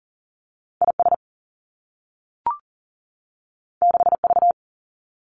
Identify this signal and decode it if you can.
Morse code, heard as a keyed tone, about 40 wpm, 700 Hz. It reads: IS 6V
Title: Two Challenging Letter-Number Combinations 40wpm